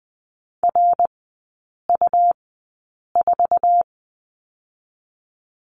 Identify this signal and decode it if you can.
Morse code, heard as a keyed tone, about 20 wpm, 700 Hz. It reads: R U 4
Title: 198 Lesson 32e - Letter k - Three Characters 20wpm